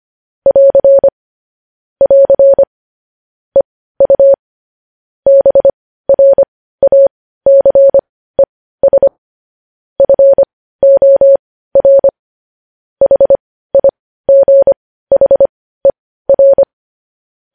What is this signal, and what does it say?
Morse code, heard as a keyed tone, about 25 wpm, spaced as for 15 wpm, 550 Hz. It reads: <AR> <AR> EU BRACES FOR HIGHER
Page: news headlines in morse code at 15 wpm